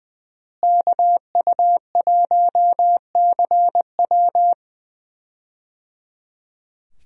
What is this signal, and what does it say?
Morse code, heard as a keyed tone, about 20 wpm, 700 Hz. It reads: KU1CW
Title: USA Call Signs - 5 Characters - Repeated 3x - 20wpm